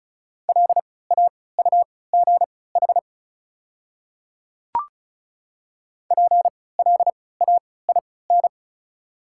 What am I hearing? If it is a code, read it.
Morse code, heard as a keyed tone, about 36 wpm, spaced as for 20 wpm, 700 Hz. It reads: LAUGH PLAIN
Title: 3x Character Spacing - Top 500 Words - 35wpm